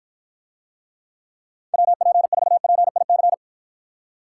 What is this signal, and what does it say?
Morse code, heard as a keyed tone, about 52 wpm, 700 Hz. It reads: WP4PIC